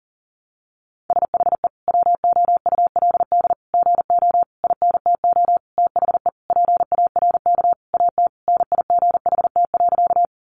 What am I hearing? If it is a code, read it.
Morse code, heard as a keyed tone, about 40 wpm, 700 Hz. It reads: SHE WOULD GO INTO THE PARK AT NIGHT.
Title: Sentences from Top 1300 Words 40wpm